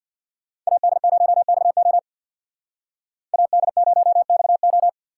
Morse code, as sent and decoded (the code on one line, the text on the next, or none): .- -.. ----- -..- --- / .- -.. ----- -..- ---
AD0XO AD0XO